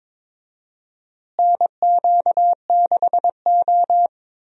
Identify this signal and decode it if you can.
Morse code, heard as a keyed tone, about 22 wpm, 700 Hz. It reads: NQ6O
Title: USA Call Signs - 4 Characters - Repeated 3x - 22wpm